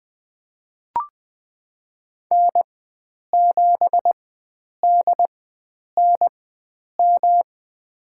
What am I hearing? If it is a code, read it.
Morse code, heard as a keyed tone, about 21 wpm, spaced as for 9 wpm, 700 Hz. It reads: N7DNM
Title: USA Call Signs - 5 Characters - 4x Character Spacing - 20wpm